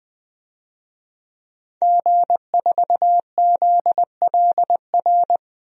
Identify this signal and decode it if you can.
Morse code, heard as a keyed tone, about 20 wpm, 700 Hz. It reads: G4ZLR
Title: DXCC-100 Call Signs - 5 Characters - 20wpm